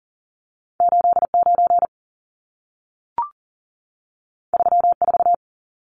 Morse code, heard as a keyed tone, about 40 wpm, 700 Hz. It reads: 89 34
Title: Numbers - 2-Digit 40wpm